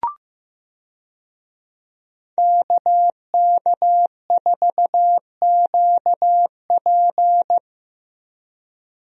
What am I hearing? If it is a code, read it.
Morse code, heard as a keyed tone, about 15 wpm, 700 Hz. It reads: KK4QP
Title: USA Call Signs - 5 Characters - Repeated 3x - 15wpm